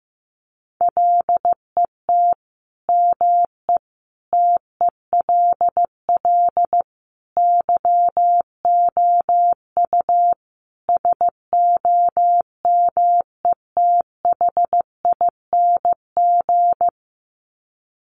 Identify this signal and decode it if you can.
Morse code, heard as a keyed tone, about 15 wpm, 700 Hz. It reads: LET ME TELL YOU SOMETHING
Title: Sets of 5 Words 15wpm